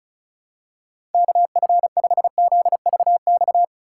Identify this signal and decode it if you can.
Morse code, heard as a keyed tone, about 34 wpm, 700 Hz. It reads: KF5ZVX
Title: USA Call Signs - 6 Characters - 4x Character Spacing - 35wpm